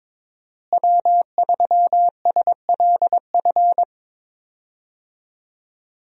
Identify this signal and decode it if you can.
Morse code, heard as a keyed tone, about 22 wpm, 700 Hz. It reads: W3SLF
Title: USA Call Signs - 5 Characters - Repeated 3x - 22wpm